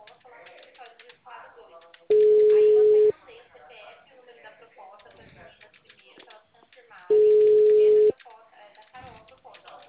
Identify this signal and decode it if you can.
Ringback tone (Europe, 425 Hz), with speech.